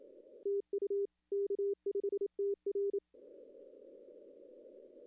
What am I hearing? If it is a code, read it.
Morse code, heard as a keyed tone, about 28 wpm, 400 Hz. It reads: TU K5TR
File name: contests / arrl-160-k5tr-op-w5tn -2300